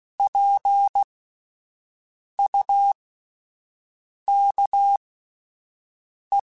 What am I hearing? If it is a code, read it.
Morse code, heard as a keyed tone, about 16 wpm, spaced as for 5 wpm, 800 Hz. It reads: PUKE